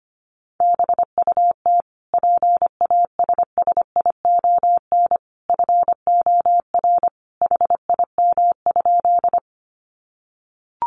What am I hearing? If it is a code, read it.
Morse code, heard as a keyed tone, about 25 wpm, 700 Hz. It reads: BUT PASSION FOR HIM?
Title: Sentences from Top 1300 Words 25wpm